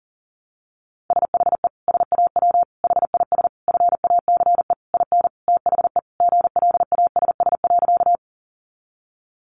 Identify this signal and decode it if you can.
Morse code, heard as a keyed tone, about 40 wpm, 700 Hz. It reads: SHE SAW HIS FACE IN THE GLASS.
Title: Sentences from Top 1300 Words 40wpm